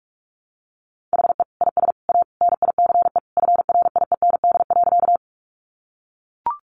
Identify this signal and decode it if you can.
Morse code, heard as a keyed tone, about 44 wpm, 700 Hz. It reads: HE IS A NICE FRIEND.
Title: Sentences from Top 1100 Words 45wpm